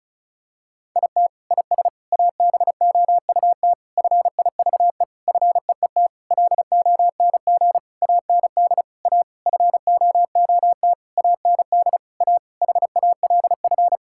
Morse code, heard as a keyed tone, about 36 wpm, 700 Hz. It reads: IT IS ABOUT FIVE FEET LONG AND A FOOT AND A HALF
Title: Sentences from Top 1000 Words - Review Part 6 - 35wpm